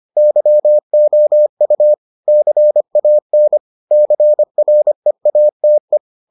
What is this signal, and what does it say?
Morse code, heard as a keyed tone, about 25 wpm, 600 Hz. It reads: YOU CAN CREATE